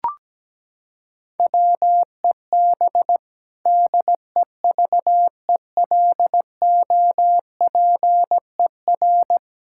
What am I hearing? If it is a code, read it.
Morse code, heard as a keyed tone, about 17 wpm, 700 Hz. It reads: WEB DEVELOPER
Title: Practice QSO - Exchange 3 - More Info - 17wpm